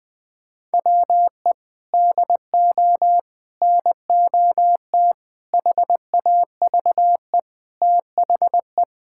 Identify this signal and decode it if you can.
Morse code, heard as a keyed tone, about 21 wpm, 700 Hz. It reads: WE DO NOT HAVE THE
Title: Build A Phrase - 5-Word Phrases - 20wpm